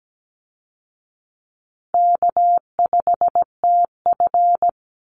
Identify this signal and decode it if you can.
Morse code, heard as a keyed tone, about 17 wpm, 700 Hz. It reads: K5TF
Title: Call Signs Repeated 2x - 17wpm